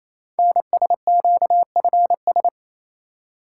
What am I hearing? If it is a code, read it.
Morse code, heard as a keyed tone, about 27 wpm, 700 Hz. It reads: NSQFS
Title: Code Groups - 5 Letters - 28wpm